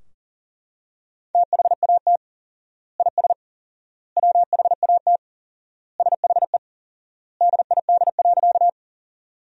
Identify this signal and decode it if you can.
Morse code, heard as a keyed tone, about 40 wpm, 700 Hz. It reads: THAT IS WHAT SHE DID.
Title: 4x Word-Spacing - Sentences from Top 100 Words - 40wpm